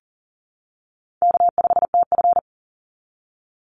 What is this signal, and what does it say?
Morse code, heard as a keyed tone, about 37 wpm, 700 Hz. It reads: K5TF
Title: Easy Call Signs - 40wpm